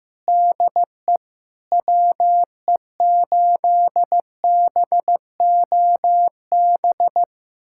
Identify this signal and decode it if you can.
Morse code, heard as a keyed tone, about 15 wpm, 700 Hz. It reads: DE WE8BOB
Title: USA Callsign DE Callsign 15wpm